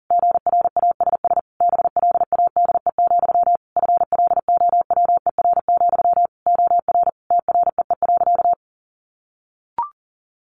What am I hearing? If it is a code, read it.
Morse code, heard as a keyed tone, about 40 wpm, 700 Hz. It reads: GRASS BLADE, FLOWER, OR TREE.
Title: A Sound Of Thunder 40wpm